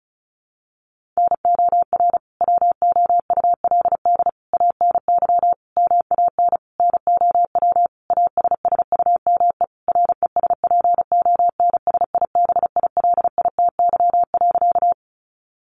Morse code, heard as a keyed tone, about 34 wpm, 700 Hz. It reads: NOR WOULD ANY MAN NOW ASSUME RESPONSIBILITY.
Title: Sentences from Top 1500 Words 35wpm